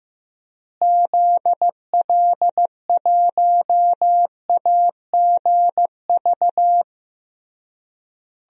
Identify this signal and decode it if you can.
Morse code, heard as a keyed tone, about 16 wpm, 700 Hz. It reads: ZL1AGV
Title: DXCC-100 Call Signs - 6 Characters - Repeated 3x - 15wpm